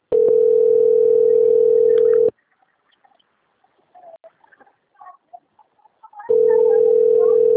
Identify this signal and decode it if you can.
Ringback tone (North America), 440 + 480 Hz.